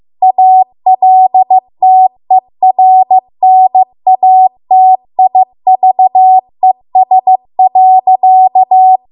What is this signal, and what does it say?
Morse code, heard as a keyed tone, about 15 wpm, 750 Hz. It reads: ALTERNATIVES.
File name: Archive / 15 WPM / 15WPM